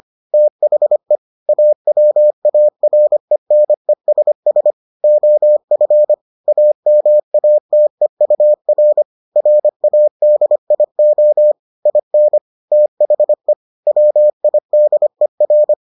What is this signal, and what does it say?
Morse code, heard as a keyed tone, about 25 wpm, 600 Hz. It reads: THE AWARENESS OF AMATEUR RADIO IN THE WIDER